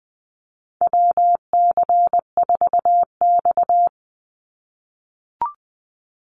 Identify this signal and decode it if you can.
Morse code, heard as a keyed tone, about 20 wpm, 700 Hz. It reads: WC4X
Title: Call Signs - 20wpm